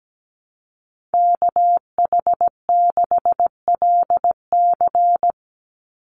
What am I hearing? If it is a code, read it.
Morse code, heard as a keyed tone, about 17 wpm, 700 Hz. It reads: KH6LC
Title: Call Signs Repeated 3x - 17wpm